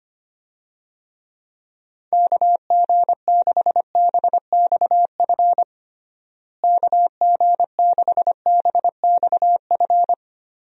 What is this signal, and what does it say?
Morse code, heard as a keyed tone, about 25 wpm, 700 Hz. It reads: KG6BXF KG6BXF
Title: USA Call Signs - 6 Characters - Repeated 3x - 25wpm